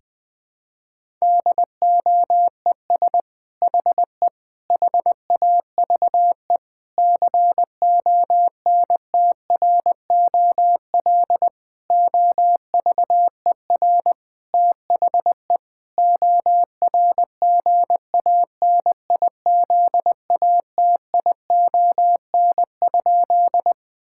Morse code, heard as a keyed tone, about 21 wpm, 700 Hz. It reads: DOES HE HAVE CONTROL OVER THE ORGANIZATION?
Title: Sentences from Top 1000 Words - Review Part 5 - 20wpm